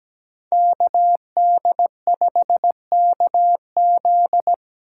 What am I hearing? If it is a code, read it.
Morse code, heard as a keyed tone, about 17 wpm, 700 Hz. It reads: KD5KZ
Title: USA Call Signs - 5 Characters - 3x Character Spacing - 17wpm